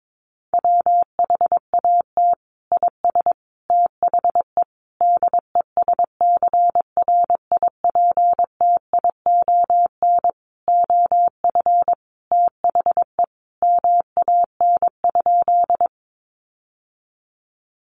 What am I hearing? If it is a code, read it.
Morse code, heard as a keyed tone, about 22 wpm, 700 Hz. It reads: WHAT IS THE DESCRIPTION OF THE MAN?
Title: Sentences from Top 1300 Words 22wpm